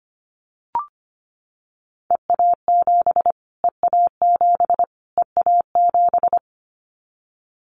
Morse code, heard as a keyed tone, about 25 wpm, 700 Hz. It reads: EA7 EA7 EA7
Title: Callsign Prefixes for Top 25 DXCC - Repeated 3x - 25wpm